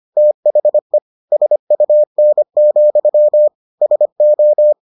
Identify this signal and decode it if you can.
Morse code, heard as a keyed tone, about 25 wpm, 600 Hz. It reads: THE SUN, SO